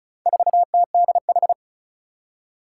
Morse code, heard as a keyed tone, about 36 wpm, 700 Hz. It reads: 4TDH
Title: Four Challenging Letter-Number Combinations 35wpm